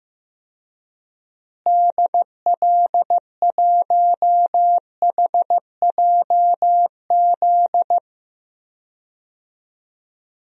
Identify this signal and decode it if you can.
Morse code, heard as a keyed tone, about 15 wpm, 700 Hz. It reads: DL1HJZ
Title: DXCC-100 Call Signs - 6 Characters - Repeated 3x - 15wpm